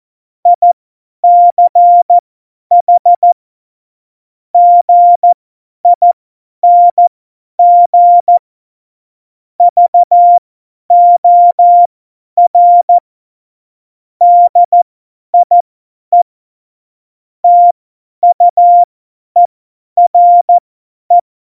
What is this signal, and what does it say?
Morse code, heard as a keyed tone, about 14 wpm, spaced as for 10 wpm, 700 Hz. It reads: ICH GING VOR DIE TUERE
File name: Morsekurs / Witze / Witz 14. November 2017, 50 F70